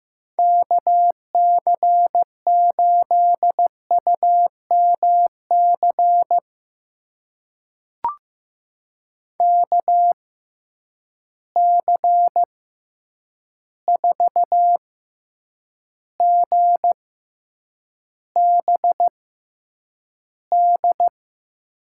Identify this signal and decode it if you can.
Morse code, heard as a keyed tone, about 15 wpm, spaced as for 8 wpm, 700 Hz. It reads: KC8UMC KC4GBD
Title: USA Call Signs - 6 Characters - 6x Character Spacing - 15wpm